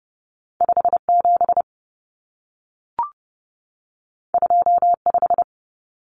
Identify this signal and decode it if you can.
Morse code, heard as a keyed tone, about 30 wpm, 700 Hz. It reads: 57 25
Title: Common Cut Numbers - 2-Digit 30wpm